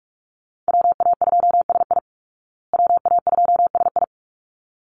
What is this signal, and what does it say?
Morse code, heard as a keyed tone, about 44 wpm, 700 Hz. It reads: WA2SI WA2SI
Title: Call Signs Repeated 3x - 45wpm